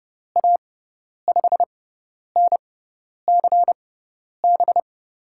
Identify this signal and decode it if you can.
Morse code, heard as a keyed tone, about 31 wpm, spaced as for 10 wpm, 700 Hz. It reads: A5NCB
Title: USA Call Signs - 6 Characters - 6x Character Spacing - 30wpm